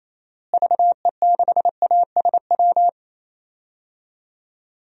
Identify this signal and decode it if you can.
Morse code, heard as a keyed tone, about 27 wpm, 700 Hz. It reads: VE6ASW
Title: DXCC-100 Call Signs - 6 Characters - Repeated 2x - 28wpm